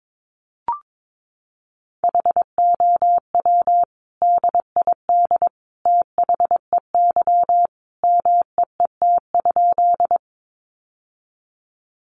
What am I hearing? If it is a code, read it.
Morse code, heard as a keyed tone, about 22 wpm, 700 Hz. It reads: HOW DID THEY MEET?
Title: Sentences from Top 1100 Words 22wpm